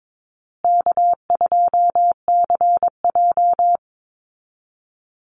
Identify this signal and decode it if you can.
Morse code, heard as a keyed tone, about 22 wpm, 700 Hz. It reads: K2CJ
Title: Call Signs Repeated 2x - 22wpm